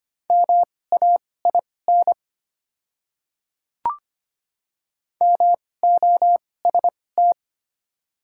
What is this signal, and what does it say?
Morse code, heard as a keyed tone, about 25 wpm, spaced as for 18 wpm, 700 Hz. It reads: MAIN MOST
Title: 2x Character Spacing - Top 300 Words - 25wpm